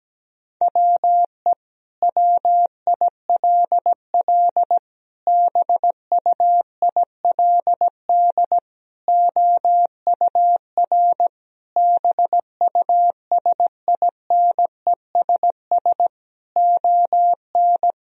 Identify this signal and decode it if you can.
Morse code, heard as a keyed tone, about 17 wpm, 700 Hz. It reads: WE WILL BUILD OUR BUSINESS ON
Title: Sentences from Top 1000 Words - Review Part 4 - 17wpm